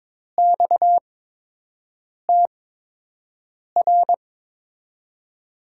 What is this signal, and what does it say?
Morse code, heard as a keyed tone, about 22 wpm, spaced as for 6 wpm, 700 Hz. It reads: XTR
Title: USA Call Signs - 5 Characters - 8x Character Spacing - 22wpm